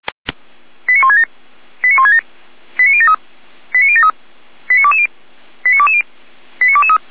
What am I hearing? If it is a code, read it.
Five-tone selective call (ZVEI), heard as quick runs of five tones, tones of about 70 ms. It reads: 89167 89167 89073 89073 89220 89220 89223